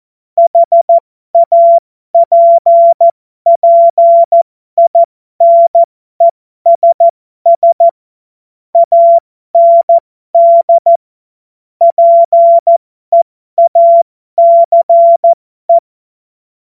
Morse code, heard as a keyed tone, about 14 wpm, spaced as for 12 wpm, 680 Hz. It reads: HAPPINESS AND PEACE
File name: Morsekurs / Witze